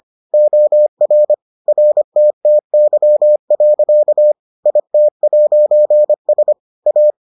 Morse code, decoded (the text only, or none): OR RTTY. IT'S A